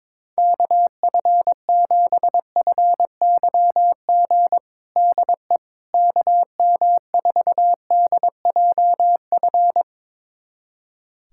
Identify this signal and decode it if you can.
Morse code, heard as a keyed tone, about 22 wpm, 700 Hz. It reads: KF7FYG DE KM4DJF
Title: USA Callsign DE Callsign 22wpm